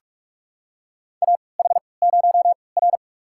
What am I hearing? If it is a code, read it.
Morse code, heard as a keyed tone, about 46 wpm, spaced as for 25 wpm, 700 Hz. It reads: AH0R
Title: USA Call Signs - 4 Characters - 3x Character Spacing - 45wpm